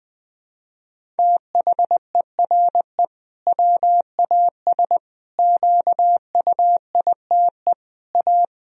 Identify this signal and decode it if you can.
Morse code, heard as a keyed tone, about 20 wpm, 700 Hz. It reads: THERE WAS QUITE A
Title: Sentences from Top 1000 Words - Review Part 1 - 20wpm